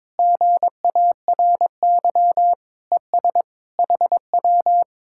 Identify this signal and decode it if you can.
Morse code, heard as a keyed tone, about 22 wpm, 700 Hz. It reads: GARY ES HW
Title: Speed-Racing Common QSO Phrases Encore - 22wpm